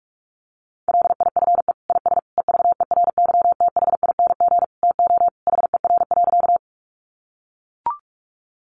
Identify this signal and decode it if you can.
Morse code, heard as a keyed tone, about 44 wpm, 700 Hz. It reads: LIFE IS EVERYTHING TO HER.
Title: Sentences from Top 700 Words 45wpm